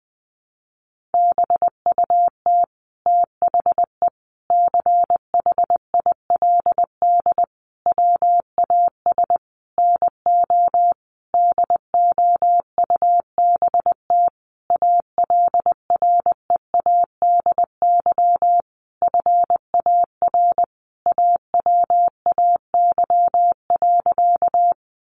Morse code, heard as a keyed tone, about 19 wpm, 700 Hz. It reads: BUT THE CHILD WAS NO DOUBT ALREADY FAR AWAY.
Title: Sentences from Top 1500 Words 20wpm